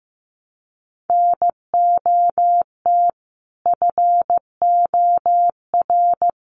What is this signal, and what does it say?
Morse code, heard as a keyed tone, about 14 wpm, 700 Hz. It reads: NOT FOR
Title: Sentences from Top 900 Words 15wpm